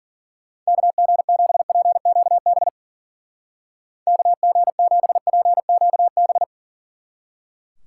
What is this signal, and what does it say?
Morse code, heard as a keyed tone, about 42 wpm, 700 Hz. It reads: KG7PQB KG7PQB
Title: Speed Racing - USA 2x3 Call Signs 40wpm